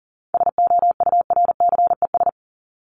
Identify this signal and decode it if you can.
Morse code, heard as a keyed tone, about 40 wpm, 700 Hz. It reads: SOURCES
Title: ES:S Suffix - Top 100 Words with Base Words 40wpm